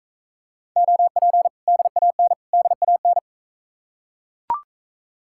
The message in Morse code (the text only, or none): OP DAN DAN